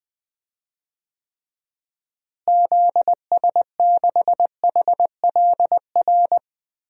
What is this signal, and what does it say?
Morse code, heard as a keyed tone, about 21 wpm, 700 Hz. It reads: ZS6HLR
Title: DXCC-100 Call Signs - 6 Characters - Repeated 3x - 20wpm